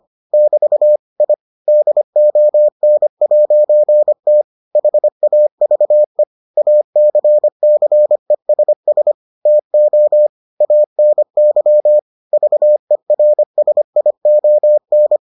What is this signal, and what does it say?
Morse code, heard as a keyed tone, about 25 wpm, 600 Hz. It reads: <BT> I DON'T HAVE ACCESS TO ANY VERSION